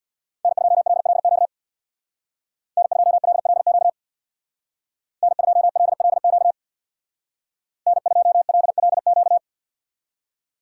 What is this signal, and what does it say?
Morse code, heard as a keyed tone, about 64 wpm, 700 Hz. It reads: N2LLQ N2LLQ N2LLQ N2LLQ